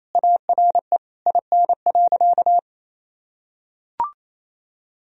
Morse code, heard as a keyed tone, about 28 wpm, 700 Hz. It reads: ARE IN.
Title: Sentences from Top 1000 Words - Review Part 2 - 28wpm